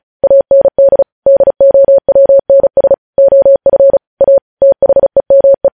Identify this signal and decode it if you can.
Morse code, heard as a keyed tone, about 36 wpm, 550 Hz. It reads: AND DOWNS OF A THEME
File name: podcast / News headlines-1x-35x35